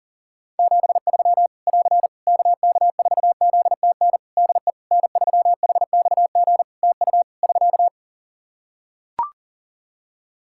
Morse code, heard as a keyed tone, about 40 wpm, 700 Hz. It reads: 73 <AR> KK4ZTN DE N3HXG TU <SK>
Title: Practice QSO - Exchange 4 - Ending - 40wpm